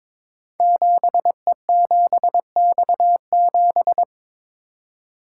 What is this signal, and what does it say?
Morse code, heard as a keyed tone, about 22 wpm, 700 Hz. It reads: 7E7X7
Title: Code Groups - 5 Letter:Numbers - 22wpm